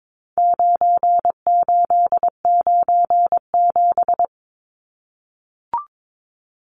Morse code, heard as a keyed tone, about 22 wpm, 700 Hz. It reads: 9897
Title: Numbers - 4-Digit 22wpm